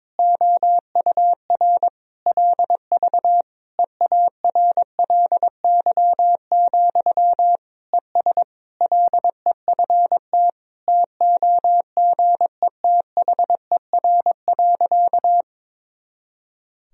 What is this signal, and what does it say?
Morse code, heard as a keyed tone, about 22 wpm, 700 Hz. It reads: OUR LV EARLY, ES LEFT TOGETHER.
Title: Sentences from Top 400 Words and Abbreviations 22wpm